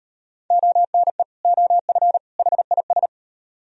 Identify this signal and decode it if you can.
Morse code, heard as a keyed tone, about 38 wpm, 700 Hz. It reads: ONE OF HIS